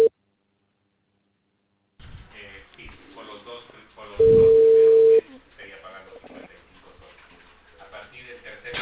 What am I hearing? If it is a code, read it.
Ringback tone (China, 450 Hz), with speech.